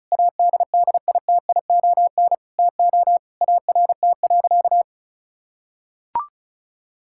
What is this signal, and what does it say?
Morse code, heard as a keyed tone, about 34 wpm, 700 Hz. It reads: ADDITION TO ART.
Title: Sentences from Top 1000 Words - Review Part 6 - 35wpm